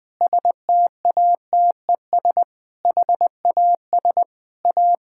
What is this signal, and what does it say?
Morse code, heard as a keyed tone, about 21 wpm, 690 Hz. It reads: STATES HAS A
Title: Sentences with 4-Word Phrases - Spoken Beginning - 20wpm